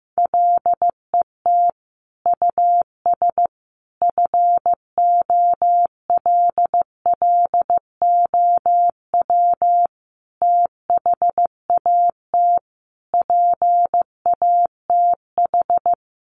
Morse code, heard as a keyed tone, about 15 wpm, 700 Hz. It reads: LET US FOLLOW THAT PATH
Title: Sentences from Top 1400 Words 15wpm